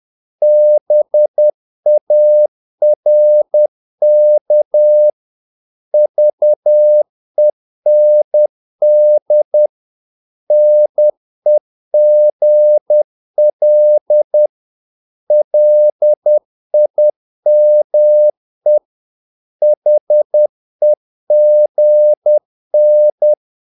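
Morse code, heard as a keyed tone, about 10 wpm, 600 Hz. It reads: BARK VEND NEGL LIME HEGN
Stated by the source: Morse practice of short Danish words at 10 wpm, from CW med Gnister.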